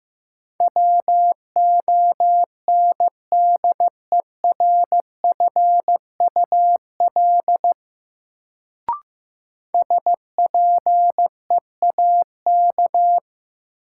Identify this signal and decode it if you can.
Morse code, heard as a keyed tone, about 15 wpm, 700 Hz. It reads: WONDERFUL SPEAK
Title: Sentences from Top 1100 Words - New Word Drill Alt - 15wpm